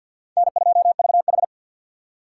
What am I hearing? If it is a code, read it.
Morse code, heard as a keyed tone, about 52 wpm, 700 Hz. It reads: N2VH
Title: USA Call Signs - 2x Character Spacing - 50wpm